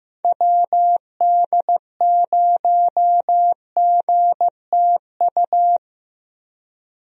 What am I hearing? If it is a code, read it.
Morse code, heard as a keyed tone, about 15 wpm, 700 Hz. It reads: WD0GTU
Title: USA Call Signs - 6 Characters - 15wpm